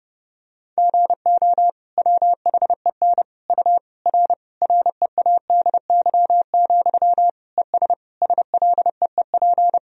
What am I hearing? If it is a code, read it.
Morse code, heard as a keyed tone, about 30 wpm, 700 Hz. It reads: GO WHEN U R READY, ES SLEEP
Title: Sentences from Top 500 Words and Abbreviations 30wpm